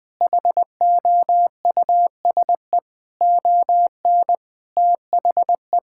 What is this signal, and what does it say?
Morse code, heard as a keyed tone, about 21 wpm, 700 Hz. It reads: HOUSE ON THE
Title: Sentences from Top 1600 Words 20wpm